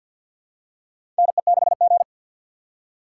Morse code, heard as a keyed tone, about 52 wpm, 700 Hz. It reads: NE6G
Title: USA Call Signs - 4 Characters - 2x Character Spacing - 50wpm